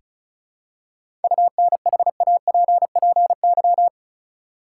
Morse code, heard as a keyed tone, about 36 wpm, 700 Hz. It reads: UNHAPPY
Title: Seven Letter Words - 35wpm